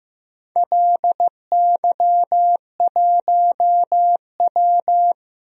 Morse code, heard as a keyed tone, about 16 wpm, 700 Hz. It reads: LY1W
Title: DXCC-100 Call Signs - 4 Characters - Repeated 3x - 15wpm